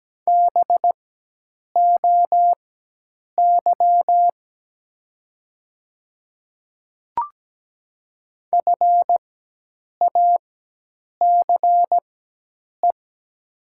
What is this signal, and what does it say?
Morse code, heard as a keyed tone, about 17 wpm, spaced as for 8 wpm, 700 Hz. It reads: BOY FACE
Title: 4x Character Spacing - Top 500 Words - 17wpm